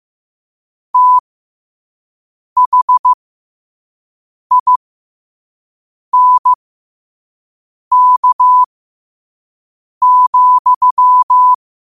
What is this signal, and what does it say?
Morse code, heard as a keyed tone, about 15 wpm, spaced as for 5 wpm, 1000 Hz. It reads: THINK,
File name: qotdfiles / 5wpm